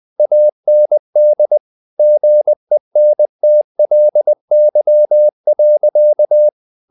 Morse code, heard as a keyed tone, about 20 wpm, 600 Hz. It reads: AND GENTLY.
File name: the old man and the sea part 2 20wpm